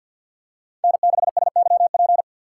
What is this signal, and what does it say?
Morse code, heard as a keyed tone, about 53 wpm, 700 Hz. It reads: N6SYP
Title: USA Call Signs - 5 Characters - 8x Character Spacing - 50wpm